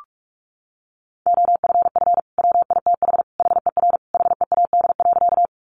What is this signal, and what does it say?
Morse code, heard as a keyed tone, about 44 wpm, 700 Hz. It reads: OFF WITH HER HEAD.
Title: Sentences from Top 300 Words 45wpm